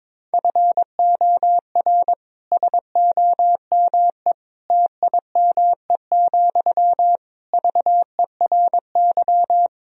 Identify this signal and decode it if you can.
Morse code, heard as a keyed tone, about 22 wpm, 700 Hz. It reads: FOR SOME TIME, VERY
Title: Sentences from Top 1000 Words - Review Part 4 - 22wpm